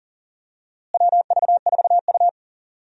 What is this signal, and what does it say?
Morse code, heard as a keyed tone, about 40 wpm, 700 Hz. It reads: WV4U
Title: USA Call Signs - 4 Characters - Repeated 3x - 40wpm